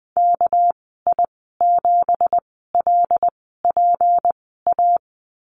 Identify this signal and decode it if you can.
Morse code, heard as a keyed tone, about 20 wpm, spaced as for 15 wpm, 700 Hz. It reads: KI7LPA
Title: 200 Lesson 32g - Letter k - Call Signs 20wpm